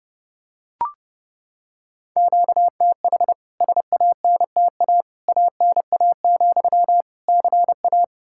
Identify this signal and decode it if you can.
Morse code, heard as a keyed tone, about 30 wpm, 700 Hz. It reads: QTH SANTA ANA, CA
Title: K1USN SST QSO Practice 30wpm